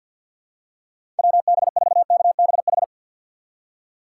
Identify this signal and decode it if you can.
Morse code, heard as a keyed tone, about 52 wpm, 700 Hz. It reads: WB4KBH
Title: USA Call Signs - 6 Characters - 6x Character Spacing - 50wpm